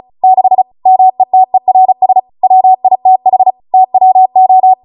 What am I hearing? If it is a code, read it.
Morse code, heard as a keyed tone, about 36 wpm, 750 Hz. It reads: 6 METERS WITH TWO